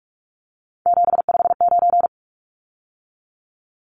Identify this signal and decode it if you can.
Morse code, heard as a keyed tone, about 44 wpm, 700 Hz. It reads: 759
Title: Numbers - 3-Digit 45wpm